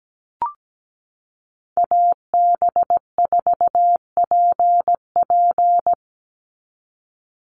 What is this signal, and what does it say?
Morse code, heard as a keyed tone, about 17 wpm, 700 Hz. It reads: AB4PP
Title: NA SOTA Chaser Callsigns 17wpm